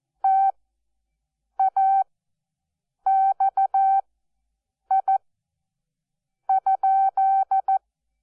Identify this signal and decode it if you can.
Morse code, heard as a keyed tone, about 14 wpm, spaced as for 6 wpm, 770 Hz. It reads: TAXI?
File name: Onlinelehrgang / morsen / DJ4UF Morsen12